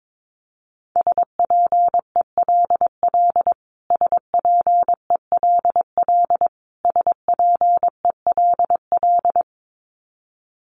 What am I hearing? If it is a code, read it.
Morse code, heard as a keyed tone, about 22 wpm, 700 Hz. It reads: SPELL SPELL SPELL
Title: Top 500 Words Repeated 22wpm